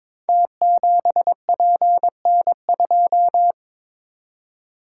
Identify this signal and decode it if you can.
Morse code, heard as a keyed tone, about 22 wpm, 690 Hz. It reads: T7PN2
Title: Code Groups - 5 Letter:Numbers - 22wpm